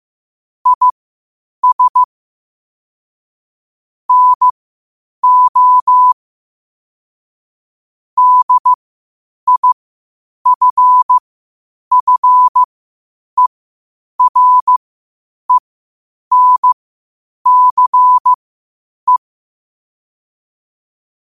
Morse code, heard as a keyed tone, about 15 wpm, spaced as for 8 wpm, 1000 Hz. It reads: IS NO DIFFERENCE
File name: qotdfiles / 8wpm